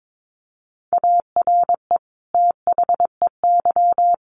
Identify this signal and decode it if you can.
Morse code, heard as a keyed tone, about 22 wpm, 700 Hz. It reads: ARE THEY
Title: Sentences from Top 1500 Words 22wpm